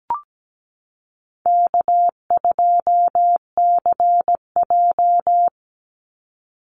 Morse code, heard as a keyed tone, about 17 wpm, 700 Hz. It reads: K2CJ
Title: Easy Call Signs Repeated 3x - 17wpm